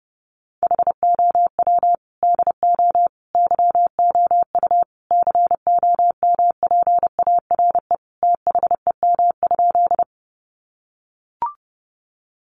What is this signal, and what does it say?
Morse code, heard as a keyed tone, about 30 wpm, 700 Hz. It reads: HOW DO YOU COMPARE THEM?
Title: Sentences from Top 1300 Words 30wpm